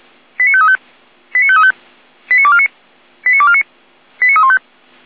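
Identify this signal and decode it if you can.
Five-tone selective call (ZVEI), heard as quick runs of five tones, tones of about 70 ms. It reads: 89535 89535 89249 89249 89315
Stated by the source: ZVEI five-tone call 89167 received by SDR.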